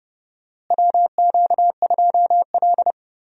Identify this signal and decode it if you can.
Morse code, heard as a keyed tone, about 31 wpm, 700 Hz. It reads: WQ2L
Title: USA Call Signs - 4 Characters - Repeated 2x - 30wpm